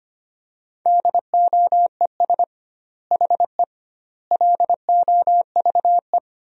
Morse code, heard as a keyed tone, about 25 wpm, 700 Hz. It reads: DOES HE LOVE
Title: 2x Word-Spacing - Sentences from Top 400 Words - 25wpm